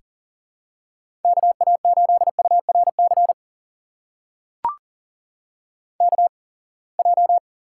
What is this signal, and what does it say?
Morse code, heard as a keyed tone, about 38 wpm, spaced as for 27 wpm, 700 Hz. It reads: KA8URC KJ
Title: USA Call Signs - 6 Characters - 8x Character Spacing - 40wpm